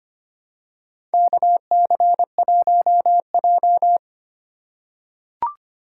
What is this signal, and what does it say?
Morse code, heard as a keyed tone, about 25 wpm, 700 Hz. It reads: KC1J
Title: USA Call Signs - 4 Characters - 2x Character Spacing - 25wpm